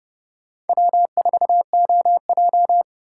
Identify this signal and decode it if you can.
Morse code, heard as a keyed tone, about 37 wpm, 700 Hz. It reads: W4OJ
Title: Speed Racing - USA 1x2 Call Signs 30wpm